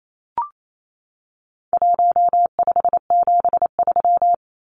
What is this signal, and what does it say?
Morse code, heard as a keyed tone, about 28 wpm, 700 Hz. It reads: 1573
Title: Numbers - Contest Style - 4-Digit 28wpm